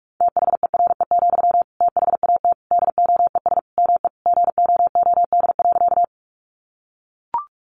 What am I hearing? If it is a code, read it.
Morse code, heard as a keyed tone, about 44 wpm, 700 Hz. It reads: THERE, THAT DOES ME GOOD.
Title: Sentences from Top 200 Words 45wpm